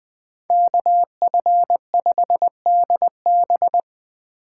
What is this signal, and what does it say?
Morse code, heard as a keyed tone, about 20 wpm, 700 Hz. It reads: KF5DB